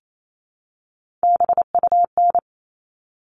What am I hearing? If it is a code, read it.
Morse code, heard as a keyed tone, about 28 wpm, 700 Hz. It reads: BUN
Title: All Cut Numbers - 3-Digit 28wpm